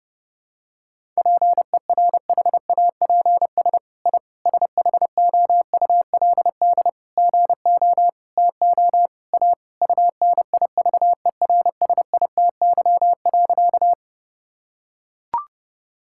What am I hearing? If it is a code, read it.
Morse code, heard as a keyed tone, about 32 wpm, 700 Hz. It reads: PERHAPS I SHOULD GO TO A UNIVERSITY.
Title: Sentences from Top 1000 Words - Review Part 5 - 30wpm